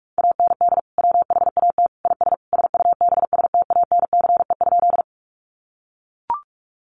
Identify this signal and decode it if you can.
Morse code, heard as a keyed tone, about 44 wpm, 700 Hz. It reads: AND WHAT IS SUBSTANCE?
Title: Sentences from Top 1500 Words 45wpm